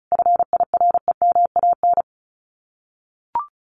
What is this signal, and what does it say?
Morse code, heard as a keyed tone, about 34 wpm, 700 Hz. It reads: FIREMAN
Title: Common QSO Phrases Encore 35wpm